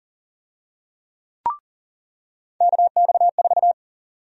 Morse code, heard as a keyed tone, about 40 wpm, 700 Hz. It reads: KX4
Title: USA Call Sign Prefixes - 40wpm